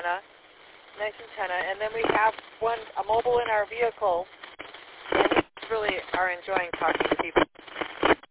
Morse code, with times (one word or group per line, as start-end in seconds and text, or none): none